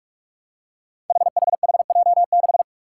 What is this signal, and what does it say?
Morse code, heard as a keyed tone, about 41 wpm, 700 Hz. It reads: HHHJ6
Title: Five Challenging Letter-Number Combinations 45wpm